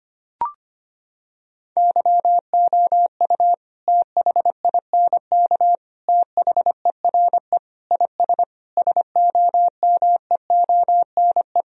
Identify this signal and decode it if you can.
Morse code, heard as a keyed tone, about 25 wpm, 700 Hz. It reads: YOU THINK THERE IS SOMEONE
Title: Sentences from Top 1000 Words - Review Part 4 - 25wpm